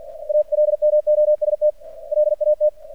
Morse code, heard as a keyed tone, about 50 wpm, 600 Hz. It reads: ALMOST LAT
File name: archived / CRSnet